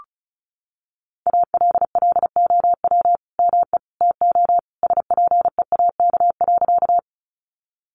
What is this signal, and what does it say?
Morse code, heard as a keyed tone, about 34 wpm, 700 Hz. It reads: ALLOW ME TO SPEAK.
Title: Sentences from Top 1100 Words 35wpm